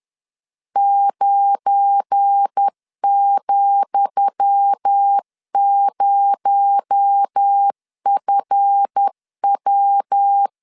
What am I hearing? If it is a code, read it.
Morse code, heard as a keyed tone, about 10 wpm, 800 Hz. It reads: 9,0FW
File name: LEZIONI / 10 SIGNS / 10D COMA 10wpm 3013